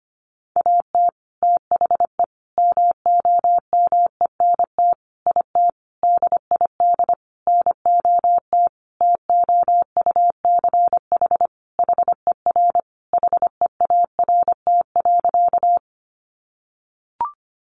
Morse code, heard as a keyed tone, about 25 wpm, 700 Hz. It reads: AT THE MOMENT IT DID NOT TOUCH HER HEART.
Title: Sentences from Top 1200 Words 25wpm